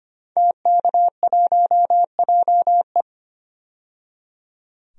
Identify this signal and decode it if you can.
Morse code, heard as a keyed tone, about 25 wpm, 700 Hz. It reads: TK1JE
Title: DXCC-100 Call Signs - 5 Characters - 25wpm